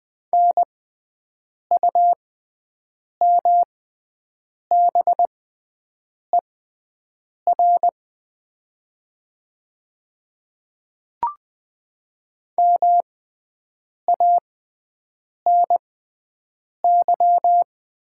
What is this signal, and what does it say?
Morse code, heard as a keyed tone, about 20 wpm, spaced as for 7 wpm, 700 Hz. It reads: NUMBER MANY
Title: 6x Character Spacing - Top 100 Words - 20wpm